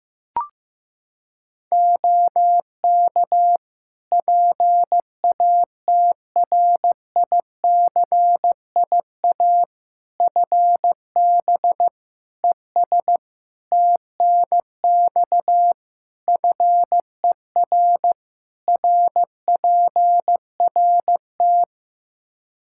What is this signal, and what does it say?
Morse code, heard as a keyed tone, about 15 wpm, 700 Hz. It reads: OK PATRICIA FB ES TNX FER RPRT
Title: Practice QSO - Exchange 2 - Info - 15wpm